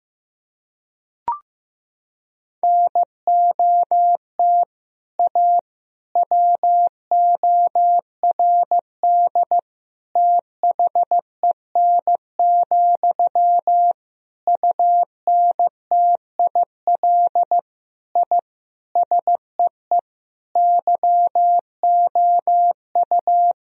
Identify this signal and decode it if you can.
Morse code, heard as a keyed tone, about 15 wpm, 700 Hz. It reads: NOT A WORD THEN, UNTIL I SEE YOU
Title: Sentences from Top 1000 Words - Review Part 1 - 15wpm